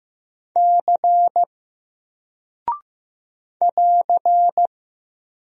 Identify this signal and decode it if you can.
Morse code, heard as a keyed tone, about 15 wpm, 700 Hz. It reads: C <AR>
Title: Single Character (Letter:Number:Punctuation:Prosign) Alt 15wpm